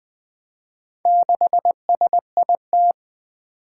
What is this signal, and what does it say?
Morse code, heard as a keyed tone, about 20 wpm, 700 Hz. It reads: 6SIT
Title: Four Challenging Letter-Number Combinations 20wpm